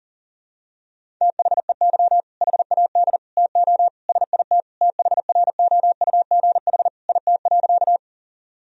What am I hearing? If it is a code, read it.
Morse code, heard as a keyed tone, about 37 wpm, 700 Hz. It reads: THEY HAD TO SIT THROUGH IT.
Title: Sentences from Top 1000 Words - Review Part 1 - 40wpm